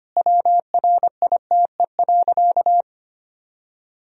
Morse code, heard as a keyed tone, about 25 wpm, 700 Hz. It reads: WRITE.
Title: Sentences from Top 1000 Words - Review Part 1 - 25wpm